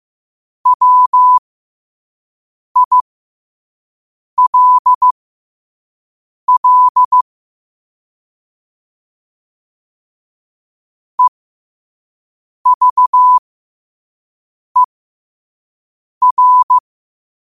Morse code, heard as a keyed tone, about 15 wpm, spaced as for 5 wpm, 1000 Hz. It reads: WILL EVER
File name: qotdfiles / 5wpm